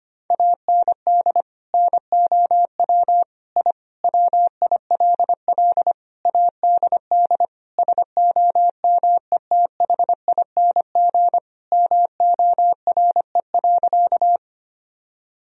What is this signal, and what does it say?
Morse code, heard as a keyed tone, about 25 wpm, 700 Hz. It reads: AND NOW I WILL ADD SOMETHING MORE.
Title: Sentences from Top 1000 Words - Review Part 1 - 25wpm